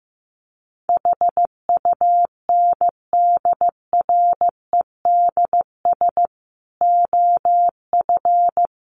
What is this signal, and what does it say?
Morse code, heard as a keyed tone, about 15 wpm, 700 Hz. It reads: HUNDREDS OF
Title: Sets of 5 Words Encore 15wpm